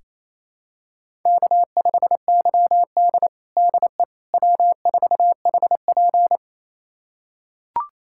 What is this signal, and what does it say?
Morse code, heard as a keyed tone, about 28 wpm, 700 Hz. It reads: K5YD DE W4HP
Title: USA Callsign DE Callsign - 4 Characters - 28wpm